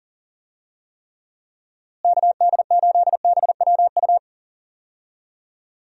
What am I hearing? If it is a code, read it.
Morse code, heard as a keyed tone, about 40 wpm, 700 Hz. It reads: KD8BWU
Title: USA Call Signs - 6 Characters - Repeated 2x - 40wpm